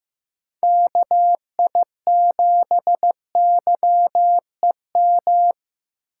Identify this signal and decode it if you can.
Morse code, heard as a keyed tone, about 15 wpm, 700 Hz. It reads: KI7YEM
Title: USA Call Signs - 6 Characters - 3x Character Spacing - 15wpm